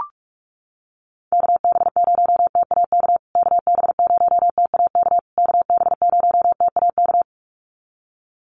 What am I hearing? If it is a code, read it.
Morse code, heard as a keyed tone, about 44 wpm, 700 Hz. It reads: KB0TAK KB0TAK KB0TAK
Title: USA Call Signs Repeated 3x 45wpm